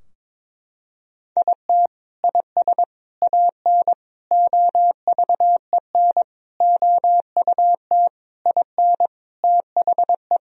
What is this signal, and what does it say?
Morse code, heard as a keyed tone, about 22 wpm, 700 Hz. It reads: IT IS AN OVEN OUT IN THE
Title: Sentences from Top 1000 Words - Review Part 4 - 22wpm